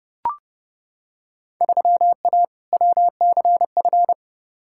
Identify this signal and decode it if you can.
Morse code, heard as a keyed tone, about 30 wpm, 700 Hz. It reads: 3A WCF
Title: Field Day Exchanges Encore - 30wpm